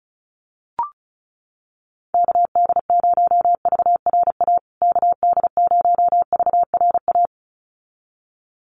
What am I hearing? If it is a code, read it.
Morse code, heard as a keyed tone, about 34 wpm, 700 Hz. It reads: KD0VRA KD0VRA
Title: USA Call Signs Repeated 2x 35wpm